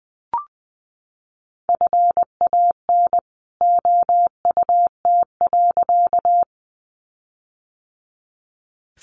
Morse code, heard as a keyed tone, about 20 wpm, 700 Hz. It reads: FAN OUT.
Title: Sentences from Top 1300 Words 20wpm